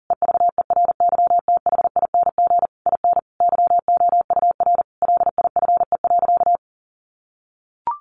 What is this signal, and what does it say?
Morse code, heard as a keyed tone, about 37 wpm, 700 Hz. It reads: EVERYTHING IN YOUR LIFE.
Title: Sentences from Top 700 Words 40wpm